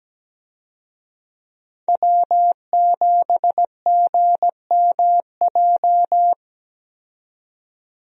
Morse code, heard as a keyed tone, about 17 wpm, 700 Hz. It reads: W7GMJ
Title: USA Call Signs - 5 Characters - 17wpm